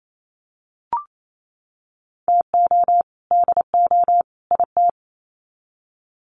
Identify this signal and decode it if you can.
Morse code, heard as a keyed tone, about 28 wpm, 700 Hz. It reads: TO DO IT
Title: Sets of 3 Words 28wpm